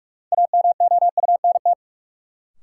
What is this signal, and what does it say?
Morse code, heard as a keyed tone, about 44 wpm, 700 Hz. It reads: AMOUNT
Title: Sentences from Top 700 Words - New Word Drill - 45wpm